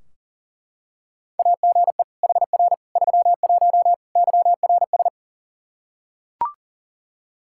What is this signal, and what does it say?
Morse code, heard as a keyed tone, about 40 wpm, 700 Hz. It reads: AGE HR 31 YRS
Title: Practice QSO - Exchange 3 - More Info - 40wpm